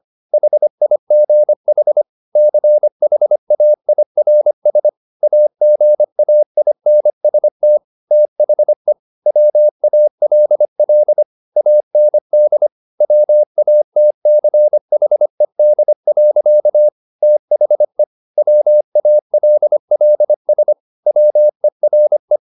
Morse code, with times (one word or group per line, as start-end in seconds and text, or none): 0.3-2.0 HIGH
2.4-4.9 CHAIRS
5.2-7.8 AGAINST
8.1-8.9 THE
9.3-11.2 WALL
11.6-12.7 AND
13.0-16.9 WATCHED.
17.2-18.0 THE
18.4-20.7 WALLS
21.1-22.4 WERE